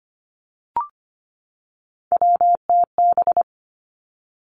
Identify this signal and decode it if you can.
Morse code, heard as a keyed tone, about 25 wpm, 700 Hz. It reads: WTB
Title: Three Challenging Letter-Number Combinations 25wpm